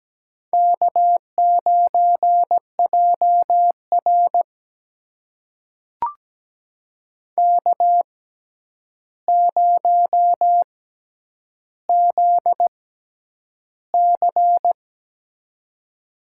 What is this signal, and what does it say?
Morse code, heard as a keyed tone, about 17 wpm, spaced as for 9 wpm, 700 Hz. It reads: K9JR K0ZC
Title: USA Call Signs - 4 Characters - 6x Character Spacing - 17wpm